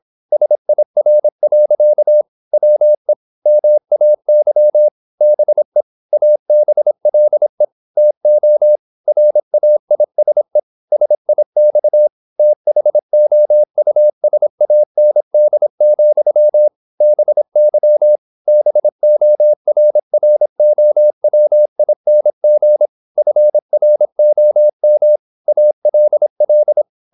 Morse code, heard as a keyed tone, about 26 wpm, 600 Hz. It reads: SIR. WE MAY BE ABLE TO RAISE SIX THOUSAND, BY BORROWING FROM ALL
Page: Never 26wpm | CW med Gnister